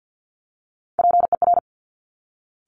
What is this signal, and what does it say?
Morse code, heard as a keyed tone, about 50 wpm, 700 Hz. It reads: PER
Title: Top Three Letter Combinations in Words 50wpm